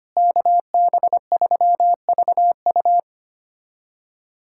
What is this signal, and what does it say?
Morse code, heard as a keyed tone, about 25 wpm, 700 Hz. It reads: KB3VU
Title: USA Callsign DE Callsign - 5 Characters - 25wpm